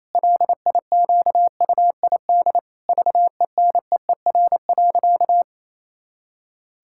Morse code, heard as a keyed tone, about 28 wpm, 700 Hz. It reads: LIQUID VENEER.
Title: Pangram Challenge v2 - 28wpm